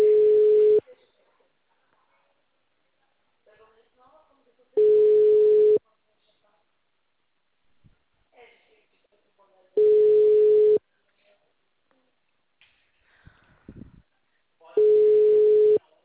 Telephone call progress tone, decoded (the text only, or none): ringback tone